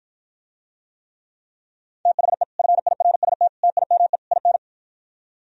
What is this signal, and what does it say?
Morse code, heard as a keyed tone, about 52 wpm, 700 Hz. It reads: THE FIRST TIME IN